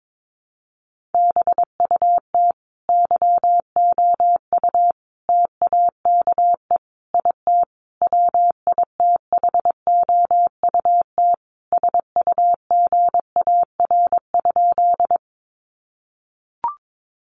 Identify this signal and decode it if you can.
Morse code, heard as a keyed tone, about 22 wpm, 700 Hz. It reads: BUT YOU TAKE IT WITHOUT SUGAR?
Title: Sentences from Top 1200 Words 22wpm